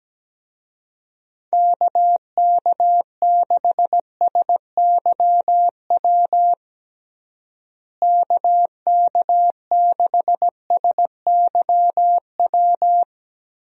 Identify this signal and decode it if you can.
Morse code, heard as a keyed tone, about 17 wpm, 700 Hz. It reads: KK6SYW KK6SYW
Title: USA Call Signs - 6 Characters - Repeated 3x - 17wpm